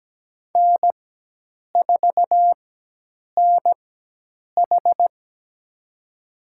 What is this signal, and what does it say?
Morse code, heard as a keyed tone, about 17 wpm, spaced as for 8 wpm, 700 Hz. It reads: N4NH
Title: USA Call Signs - 4x Character Spacing - 17wpm